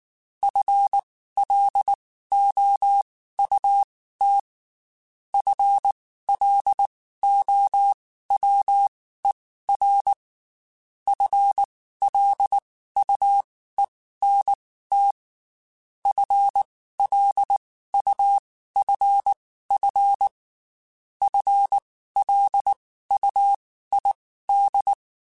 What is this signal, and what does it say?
Morse code, heard as a keyed tone, about 19 wpm, spaced as for 14 wpm, 800 Hz. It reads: FLOUT FLOWER FLUENT FLUFF FLUID
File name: Spaced / FL Runs sp mp3s / FL Runs sp @18